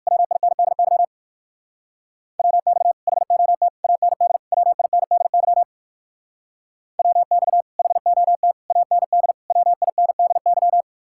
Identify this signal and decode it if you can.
Morse code, heard as a keyed tone, about 54 wpm, 700 Hz. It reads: WINDY WX HOT AND WINDY WX HOT AND WINDY
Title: Speed-Racing Common QSO Phrases - 45wpm